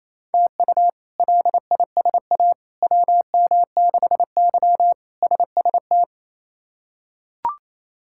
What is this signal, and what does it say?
Morse code, heard as a keyed tone, about 28 wpm, 700 Hz. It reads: TU LISA WM6Y SST
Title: K1USN SST QSO Practice - 28wpm